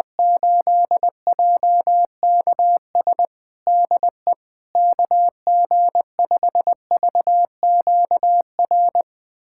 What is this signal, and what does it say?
Morse code, heard as a keyed tone, about 21 wpm, 700 Hz. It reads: 8JKS DE KG5VQR
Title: USA Callsign DE Callsign - 6 Characters - 20wpm